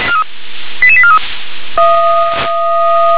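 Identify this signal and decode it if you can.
Five-tone selective call (ZVEI), heard as quick runs of five tones, tones of about 70 ms. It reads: 88843
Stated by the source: Received on SDR; ZVEI five-tone call 88842, channel 357.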